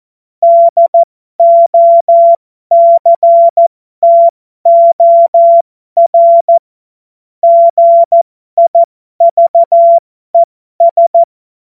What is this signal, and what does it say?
Morse code, heard as a keyed tone, about 14 wpm, spaced as for 12 wpm, 680 Hz. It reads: DOCTOR GIVES
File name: Morsekurs / Witze